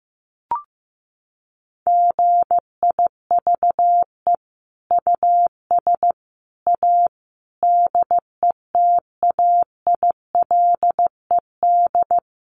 Morse code, heard as a keyed tone, about 15 wpm, 700 Hz. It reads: GIVE US A DETAILED
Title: Sentences from Top 1500 Words 15wpm